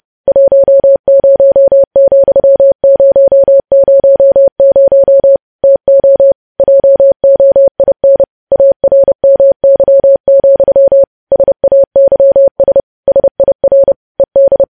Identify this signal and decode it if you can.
Morse code, heard as a keyed tone, about 30 wpm, 550 Hz. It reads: 10,000 TO JOIN ARMY, SAYS SIR ED
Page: news headlines in morse code at 30 wpm